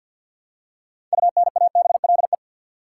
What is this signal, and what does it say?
Morse code, heard as a keyed tone, about 53 wpm, 700 Hz. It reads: UNABLE
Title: Six Letter Words - 50wpm